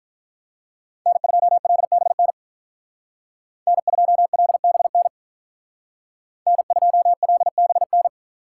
Morse code, heard as a keyed tone, about 45 wpm, 700 Hz. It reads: N2LBN N2LBN N2LBN